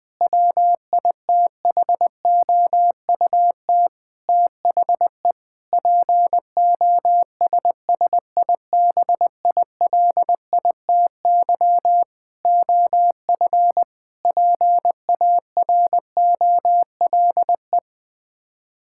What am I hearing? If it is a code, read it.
Morse code, heard as a keyed tone, about 20 wpm, 700 Hz. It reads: WITHOUT THE POSSIBILITY OF PAROLE
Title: Sets of 8 Words - 20wpm